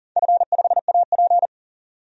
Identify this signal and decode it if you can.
Morse code, heard as a keyed tone, about 40 wpm, 700 Hz. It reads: F5AP
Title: DXCC-100 Call Signs - 4 Characters - 40wpm